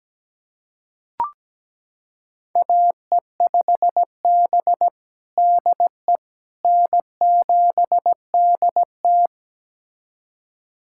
Morse code, heard as a keyed tone, about 17 wpm, 700 Hz. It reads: AE5B DE N7DT
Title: USA Callsign DE Callsign - 4 Characters - 17wpm